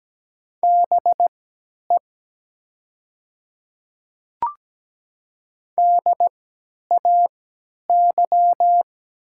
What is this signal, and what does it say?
Morse code, heard as a keyed tone, about 17 wpm, spaced as for 10 wpm, 700 Hz. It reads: BE DAY
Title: 3x Character Spacing - Top 100 Words - 17wpm